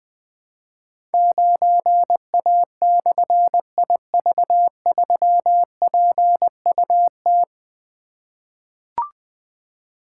Morse code, heard as a keyed tone, about 20 wpm, 700 Hz. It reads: 9A/IV3PUT
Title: Difficult Call Signs 20wpm